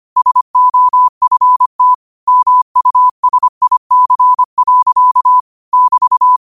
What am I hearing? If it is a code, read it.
Morse code, heard as a keyed tone, about 25 wpm, 1000 Hz. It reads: SOFT MUSIC. <BT>
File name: qotdfiles / 25wpm